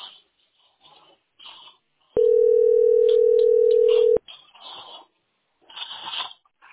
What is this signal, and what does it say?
Ringback tone (North America), 440 + 480 Hz.